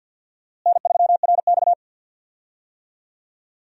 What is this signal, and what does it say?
Morse code, heard as a keyed tone, about 52 wpm, 700 Hz. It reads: N3RX
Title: USA Call Signs - 4 Characters - Repeated 3x - 50wpm